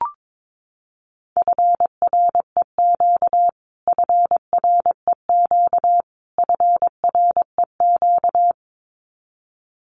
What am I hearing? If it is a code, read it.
Morse code, heard as a keyed tone, about 22 wpm, 700 Hz. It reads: FREQ FREQ FREQ
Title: Abbreviations Repeated 3x - 22wpm